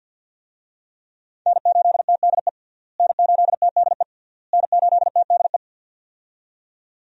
Morse code, heard as a keyed tone, about 48 wpm, 700 Hz. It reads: N8TDE N8TDE N8TDE
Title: USA Call Signs - 5 Characters - Repeated 3x - 50wpm